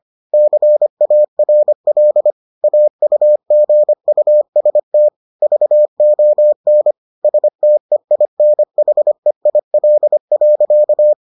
Morse code, heard as a keyed tone, about 25 wpm, 600 Hz. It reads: CARL AUGUST VON STEINHEIL.